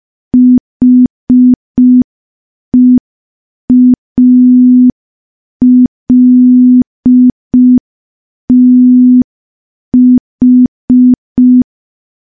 Morse code, heard as a keyed tone, about 5 wpm, 250 Hz. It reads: HEALTH
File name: morse / morse-training-05WPM